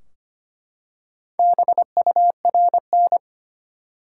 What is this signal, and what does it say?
Morse code, heard as a keyed tone, about 25 wpm, 700 Hz. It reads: BURN
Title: Four Letter Words - 25wpm